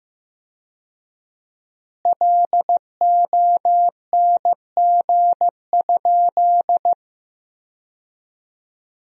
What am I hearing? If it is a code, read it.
Morse code, heard as a keyed tone, about 15 wpm, 700 Hz. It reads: LONG?
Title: 4x Word-Spacing - Sentences from Top 200 Words - 15wpm